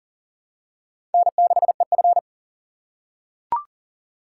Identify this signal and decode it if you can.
Morse code, heard as a keyed tone, about 40 wpm, 700 Hz. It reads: N6EF
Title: USA Call Signs - 4x Character Spacing - 40wpm